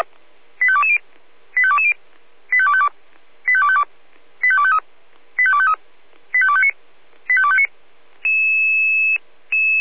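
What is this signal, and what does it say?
Five-tone selective call (ZVEI), heard as quick runs of five tones, tones of about 70 ms. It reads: 85220 85220 85252 85252 85253 85253 85259 85259
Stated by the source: ZVEI five-tone call 85264 received by SDR.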